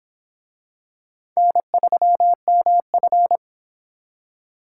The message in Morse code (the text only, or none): N3MF